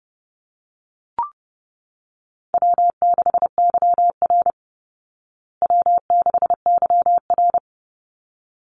Morse code, heard as a keyed tone, about 30 wpm, 700 Hz. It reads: W6YR W6YR
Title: Easy Call Signs Repeated 2x - 30wpm